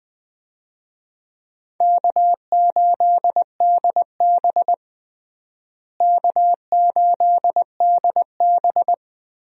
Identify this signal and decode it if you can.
Morse code, heard as a keyed tone, about 21 wpm, 700 Hz. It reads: K8DB K8DB
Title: USA Call Signs - 4 Characters - Repeated 2x - 20wpm